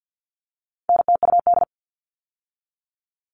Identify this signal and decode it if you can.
Morse code, heard as a keyed tone, about 50 wpm, 700 Hz. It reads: NTUD
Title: All Cut Numbers - 4-Digit 50wpm